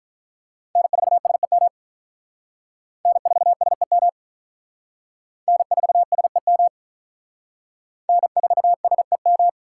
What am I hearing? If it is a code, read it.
Morse code, heard as a keyed tone, about 46 wpm, 700 Hz. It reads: N4SEM N4SEM N4SEM N4SEM